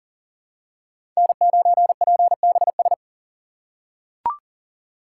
Morse code, heard as a keyed tone, about 40 wpm, 700 Hz. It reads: N9PBS
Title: USA Call Signs - 5 Characters - 4x Character Spacing - 40wpm